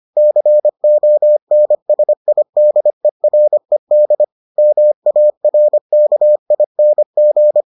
Morse code, heard as a keyed tone, about 25 wpm, 600 Hz. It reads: CONSIDERED MARKING